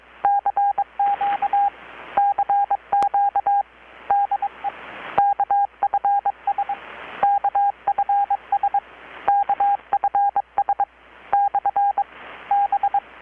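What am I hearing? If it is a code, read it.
Morse code, heard as a keyed tone, about 22 wpm, 800 Hz, with radio noise. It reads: CQ CQ DE KFS KFS KFS / B
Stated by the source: Morse from KPH's Night of Nights XX maritime radiotelegraphy event, received in Chiapas, Mexico about 3,800 km away on a Kenwood R-600 with a grounded 10 m long-wire antenna.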